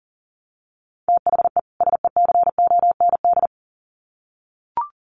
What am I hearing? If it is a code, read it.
Morse code, heard as a keyed tone, about 37 wpm, 700 Hz. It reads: THE SECOND
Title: Sets of 2 Words 40wpm